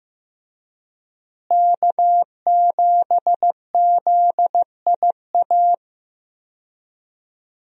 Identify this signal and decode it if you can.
Morse code, heard as a keyed tone, about 15 wpm, 700 Hz. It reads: K7ZIA
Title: USA Call Signs - 5 Characters - Repeated 2x - 15wpm